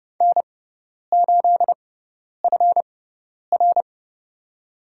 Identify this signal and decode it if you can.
Morse code, heard as a keyed tone, about 31 wpm, spaced as for 10 wpm, 700 Hz. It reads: N8FR
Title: USA Call Signs - 4 Characters - 6x Character Spacing - 30wpm